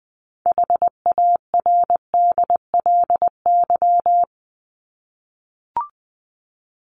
Morse code, heard as a keyed tone, about 20 wpm, 700 Hz. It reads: HARDLY
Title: Top 1751-2000 Words 20wpm